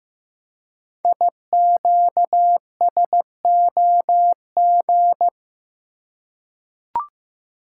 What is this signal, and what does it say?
Morse code, heard as a keyed tone, about 15 wpm, 700 Hz. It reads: IQSOG
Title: Code Groups - 5 Letters - 15wpm